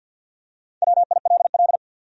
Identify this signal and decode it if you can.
Morse code, heard as a keyed tone, about 53 wpm, 700 Hz. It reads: WILL
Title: Top 100 Words Repeated 50wpm